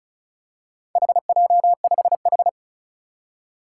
Morse code, heard as a keyed tone, about 36 wpm, 700 Hz. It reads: HJ5H
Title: Four Challenging Letter-Number Combinations 35wpm